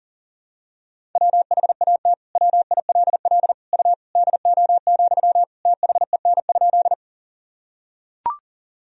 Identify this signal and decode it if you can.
Morse code, heard as a keyed tone, about 40 wpm, 700 Hz. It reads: WHAT WILL U DO, THEN?
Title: Sentences from Top 100 Words and Abbreviations 40wpm